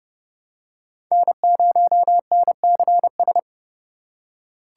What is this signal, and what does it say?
Morse code, heard as a keyed tone, about 30 wpm, 700 Hz. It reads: N0NCS
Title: USA Call Signs - 5 Characters - 2x Character Spacing - 30wpm